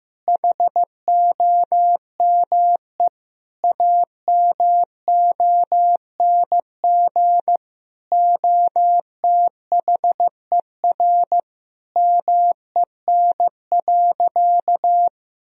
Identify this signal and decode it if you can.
Morse code, heard as a keyed tone, about 15 wpm, 700 Hz. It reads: HOME AMONG OTHER MEN.
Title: Sentences from Top 1000 Words - Review Part 3 - 15wpm